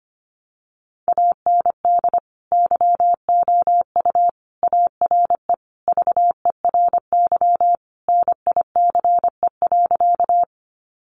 Morse code, heard as a keyed tone, about 25 wpm, 700 Hz. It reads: AND YOU ARE VERY NICE.
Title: Sentences from Top 1100 Words 25wpm